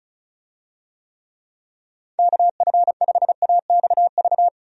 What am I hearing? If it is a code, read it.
Morse code, heard as a keyed tone, about 34 wpm, 700 Hz. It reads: KF5AXV
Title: USA Call Signs - 6 Characters - 35wpm